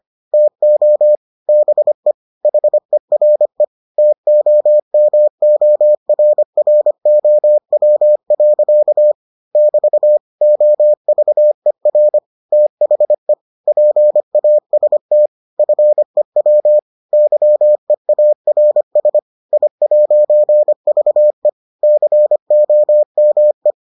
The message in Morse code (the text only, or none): TO BE HERE TOMORROW. <BT> OVER THE PAST FEW YEARS I'VE COME